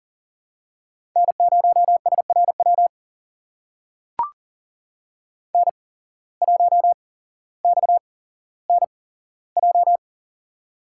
Morse code, heard as a keyed tone, about 38 wpm, spaced as for 17 wpm, 700 Hz. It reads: N0SRW N1XNJ
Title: USA Call Signs - 5 Characters - 8x Character Spacing - 40wpm